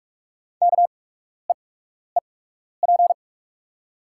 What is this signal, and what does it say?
Morse code, heard as a keyed tone, about 44 wpm, spaced as for 12 wpm, 690 Hz. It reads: KEEP
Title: 8x Character Spacing - Top 300 Words - 45wpm